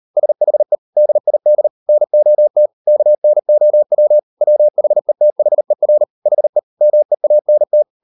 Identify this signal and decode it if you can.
Morse code, heard as a keyed tone, about 38 wpm, 600 Hz. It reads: SHE DID NOT KNOW WHETHER HE MEANT